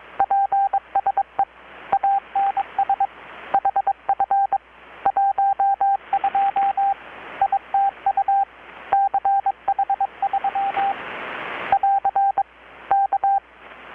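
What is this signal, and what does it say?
Morse code, heard as a keyed tone, about 22 wpm, 790 Hz, with radio noise. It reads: PSE ANS HF 12 ITU CH3 <AR> K
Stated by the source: Morse from KPH's Night of Nights XX maritime radiotelegraphy event, received in Chiapas, Mexico about 3,800 km away on a Kenwood R-600 with a grounded 10 m long-wire antenna.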